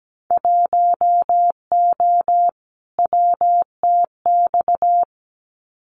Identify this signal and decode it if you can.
Morse code, heard as a keyed tone, about 17 wpm, 700 Hz. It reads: 1O WTX
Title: Winter Field Day Exchanges Encore 17wpm